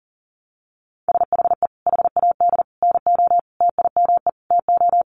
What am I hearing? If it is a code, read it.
Morse code, heard as a keyed tone, about 40 wpm, 700 Hz. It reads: SHE HAD NO TIME TO
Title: Sentences from Top 1100 Words 40wpm